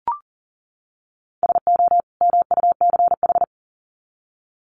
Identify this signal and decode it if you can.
Morse code, heard as a keyed tone, about 40 wpm, 700 Hz. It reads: SO MUCH
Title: Sets of 2 Words 40wpm